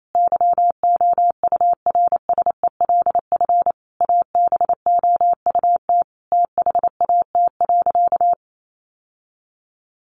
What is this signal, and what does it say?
Morse code, heard as a keyed tone, about 28 wpm, 700 Hz. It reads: YOURSELF ABOUT THAT.
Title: Sentences from Top 1200 Words 28wpm